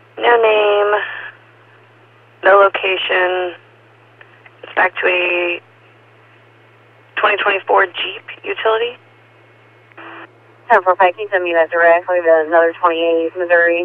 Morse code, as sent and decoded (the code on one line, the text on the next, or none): none